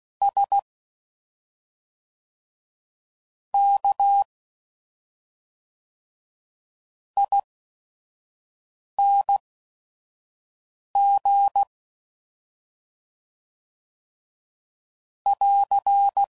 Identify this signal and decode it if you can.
Morse code, heard as a keyed tone, about 16 wpm, spaced as for 5 wpm, 800 Hz. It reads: S K ING <AR>
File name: Suffixes / Special ING's / SlowToFaster ING's